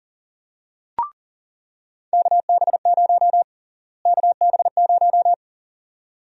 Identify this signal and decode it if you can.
Morse code, heard as a keyed tone, about 40 wpm, 700 Hz. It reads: KB0 KB0
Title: USA Call Sign Prefixes - Repeated 2x - 40wpm